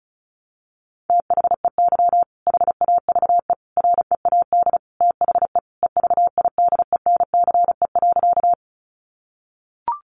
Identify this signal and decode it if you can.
Morse code, heard as a keyed tone, about 34 wpm, 700 Hz. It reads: THEY HAVE READ THE EVIDENCE.
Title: Sentences from Top 1300 Words 35wpm